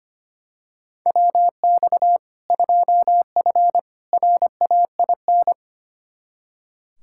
Morse code, heard as a keyed tone, about 25 wpm, 700 Hz. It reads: WX 2F RAIN
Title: K1USN SST QSO Practice 25wpm